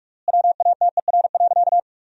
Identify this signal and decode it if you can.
Morse code, heard as a keyed tone, about 46 wpm, 700 Hz. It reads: WATER.
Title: 3x Word-Spacing - Sentences from Top 300 Words - 45wpm